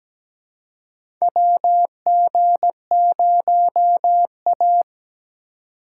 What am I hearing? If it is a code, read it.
Morse code, heard as a keyed tone, about 17 wpm, 700 Hz. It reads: WG0A
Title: USA Call Signs - 4 Characters - 3x Character Spacing - 17wpm